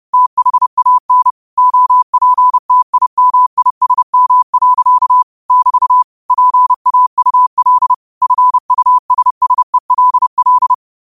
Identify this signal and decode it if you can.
Morse code, heard as a keyed tone, about 30 wpm, 1000 Hz. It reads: THAN OPTIMISM. <BT> PAUL FUSSELL
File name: qotdfiles / 30wpm